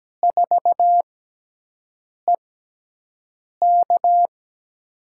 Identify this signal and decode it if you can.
Morse code, heard as a keyed tone, about 17 wpm, spaced as for 6 wpm, 700 Hz. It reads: 4EK
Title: USA Call Signs - 5 Characters - 6x Character Spacing - 17wpm